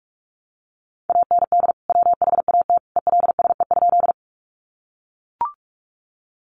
Morse code, heard as a keyed tone, about 44 wpm, 700 Hz. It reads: AND WHAT ELSE?
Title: Sentences from Top 800 Words 45wpm